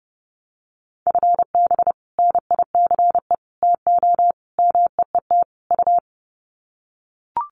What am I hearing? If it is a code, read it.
Morse code, heard as a keyed tone, about 30 wpm, 700 Hz. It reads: FB NICE TO MEET U
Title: Common QSO Phrases 30wpm